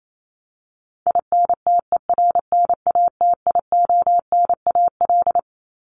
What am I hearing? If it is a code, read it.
Morse code, heard as a keyed tone, about 28 wpm, 700 Hz. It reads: INTERNATIONAL
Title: Top 501-750 Words 28wpm